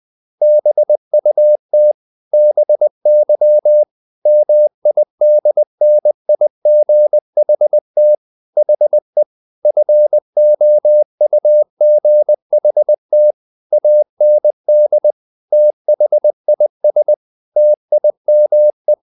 Morse code, heard as a keyed tone, about 20 wpm, 600 Hz. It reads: BUT BY MIDNIGHT HE FOUGHT AND THIS TIME
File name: the old man and the sea part 2 20wpm